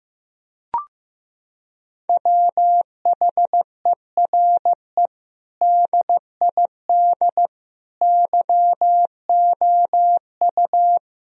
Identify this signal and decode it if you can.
Morse code, heard as a keyed tone, about 16 wpm, 700 Hz. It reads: WHERE DID YOU
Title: Sentences from Top 1000 Words - Review Part 3 - 15wpm